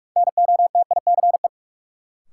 Morse code, heard as a keyed tone, about 46 wpm, 700 Hz. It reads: NOTICE
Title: Sentences from Top 400 Words - New Word Drill - 45wpm